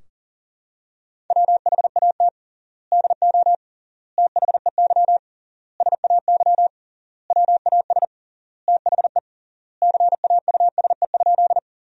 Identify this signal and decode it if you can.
Morse code, heard as a keyed tone, about 40 wpm, 700 Hz. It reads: WHAT DO THEY SAY WAS THE CAUSE?
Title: 3x Word-Spacing - Sentences from Top 200 Words - 40wpm